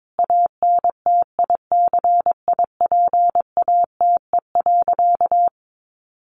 Morse code, heard as a keyed tone, about 22 wpm, 700 Hz. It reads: ANTICIPATE.
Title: Sentences from Top 1500 Words 22wpm